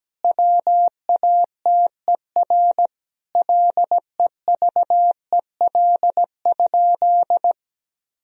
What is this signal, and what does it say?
Morse code, heard as a keyed tone, about 17 wpm, 700 Hz. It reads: WATER LEVEL?
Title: Sentences from Top 1000 Words - Review Part 4 - 17wpm